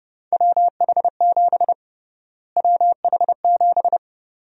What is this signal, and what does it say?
Morse code, heard as a keyed tone, about 30 wpm, 700 Hz. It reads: WH7 WH7
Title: USA Call Sign Prefixes - Repeated 2x - 30wpm